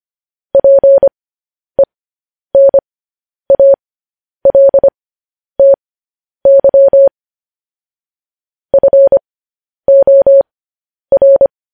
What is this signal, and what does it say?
Morse code, heard as a keyed tone, about 25 wpm, spaced as for 10 wpm, 550 Hz. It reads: PENALTY FOR
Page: news headlines in morse code at 10 wpm